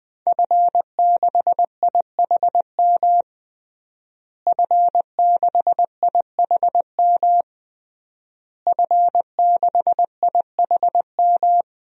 Morse code, heard as a keyed tone, about 20 wpm, 700 Hz. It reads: F6IHM F6IHM F6IHM
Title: DXCC-100 Call Signs - 5 Characters - Repeated 3x - 20wpm